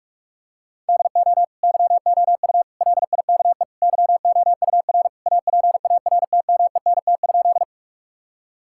Morse code, heard as a keyed tone, about 46 wpm, 700 Hz. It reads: DO YOU LIKE YOUR APARTMENT?
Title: Sentences from Top 1000 Words - Review Part 6 - 45wpm